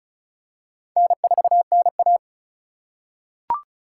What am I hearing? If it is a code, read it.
Morse code, heard as a keyed tone, about 36 wpm, 700 Hz. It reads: N4NA
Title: USA Call Signs - 3x Character Spacing - 35wpm